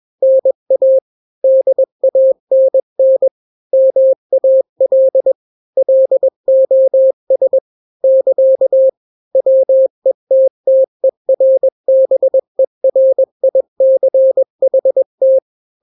Morse code, heard as a keyed tone, about 21 wpm, 530 Hz. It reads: NA DANN MAL LOS <KA> WETTERBERICHT
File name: daily / 20wpm